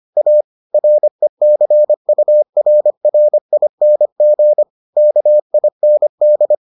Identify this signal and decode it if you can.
Morse code, heard as a keyed tone, about 25 wpm, 600 Hz. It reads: A RECURRING KIND